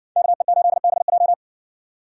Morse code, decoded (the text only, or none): KE8BJ